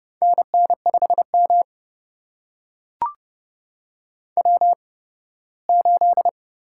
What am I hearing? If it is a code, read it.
Morse code, heard as a keyed tone, about 32 wpm, spaced as for 18 wpm, 700 Hz. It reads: NN5M W8
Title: USA Call Signs - 4 Characters - 8x Character Spacing - 30wpm